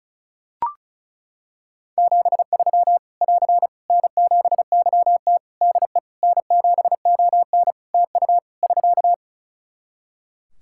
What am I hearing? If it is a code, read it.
Morse code, heard as a keyed tone, about 34 wpm, 700 Hz. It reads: 73 <AR> N7YT DE N7ON TU <SK>
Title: Practice QSO - Exchange 4 - Ending - 35wpm